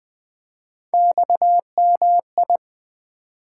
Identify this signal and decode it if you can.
Morse code, heard as a keyed tone, about 20 wpm, 700 Hz. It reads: XMI
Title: Three Letter-Number Combinations - 20wpm